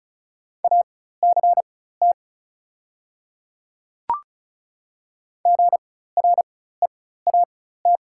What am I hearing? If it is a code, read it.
Morse code, heard as a keyed tone, about 36 wpm, spaced as for 16 wpm, 690 Hz. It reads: ACT GREAT
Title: 4x Character Spacing - Top 500 Words - 35wpm